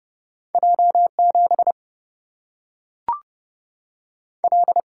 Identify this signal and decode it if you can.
Morse code, heard as a keyed tone, about 30 wpm, 700 Hz. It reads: J 7 L
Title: Single-Letter Single-Number 30wpm